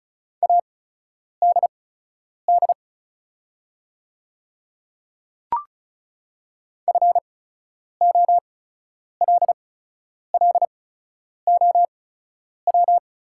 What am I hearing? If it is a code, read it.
Morse code, heard as a keyed tone, about 36 wpm, spaced as for 10 wpm, 690 Hz. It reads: ADD FOLLOW
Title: 8x Character Spacing - Top 200 Words - 35wpm